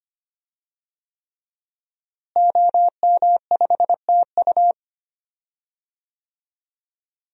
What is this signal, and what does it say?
Morse code, heard as a keyed tone, about 25 wpm, 700 Hz. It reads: OM5TU
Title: DXCC-100 Call Signs - 5 Characters - Repeated 3x - 25wpm